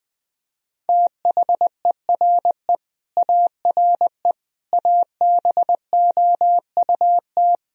Morse code, heard as a keyed tone, about 20 wpm, 700 Hz. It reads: THERE ARE ABOUT
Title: Sentences from Top 1000 Words - Review Part 2 - 20wpm